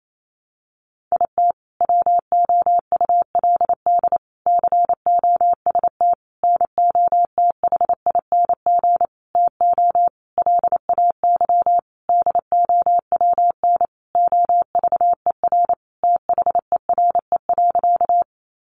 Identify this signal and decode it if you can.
Morse code, heard as a keyed tone, about 28 wpm, 700 Hz. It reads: IT WOULD COST NOTHING TO LAY DOWN OVER THERE.
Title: Sentences from Top 800 Words 28wpm